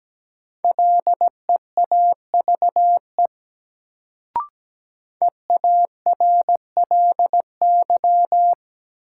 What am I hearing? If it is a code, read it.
Morse code, heard as a keyed tone, about 17 wpm, 700 Hz. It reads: LEAVE EARLY
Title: Sentences from Top 400 Words - New Word Drill - 17wpm